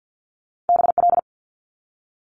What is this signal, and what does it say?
Morse code, heard as a keyed tone, about 50 wpm, 700 Hz. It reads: BL
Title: Multi-letter Phonemes 50wpm